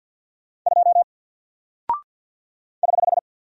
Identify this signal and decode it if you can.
Morse code, heard as a keyed tone, about 48 wpm, 700 Hz. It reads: 2 <HH>
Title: Single Character (Letter:Number:Punctuation:Prosign) 50wpm